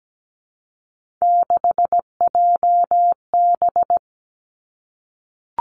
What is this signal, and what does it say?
Morse code, heard as a keyed tone, about 17 wpm, 700 Hz. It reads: 6JB
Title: Three Challenging Letter-Number Combinations 17wpm